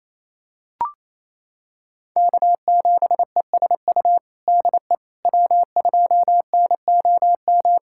Morse code, heard as a keyed tone, about 28 wpm, 700 Hz. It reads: K7ESU DE W2NOM
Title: USA Callsign DE Callsign - 5 Characters - 28wpm